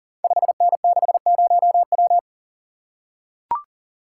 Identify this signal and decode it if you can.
Morse code, heard as a keyed tone, about 39 wpm, 690 Hz. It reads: 5N60W
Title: Code Groups - 5 Letter:Numbers - 40wpm